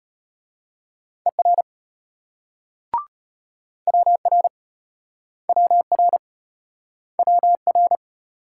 Morse code, heard as a keyed tone, about 34 wpm, 700 Hz. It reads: ER WR WR WR